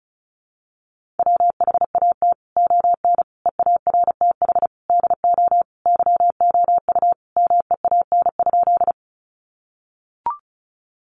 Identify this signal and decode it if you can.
Morse code, heard as a keyed tone, about 34 wpm, 700 Hz. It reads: WHAT ON EARTH DO YOU MEAN?
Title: Sentences from Top 200 Words 35wpm